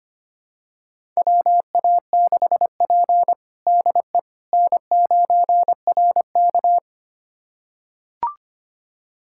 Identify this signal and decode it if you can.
Morse code, heard as a keyed tone, about 25 wpm, 700 Hz. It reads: WA6P DE N9RK
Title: USA Callsign DE Callsign - 4 Characters - 25wpm